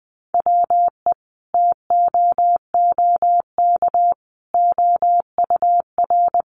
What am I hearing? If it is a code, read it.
Morse code, heard as a keyed tone, about 20 wpm, 700 Hz. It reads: WE TOOK OUR
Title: Sentences from Top 400 Words 20wpm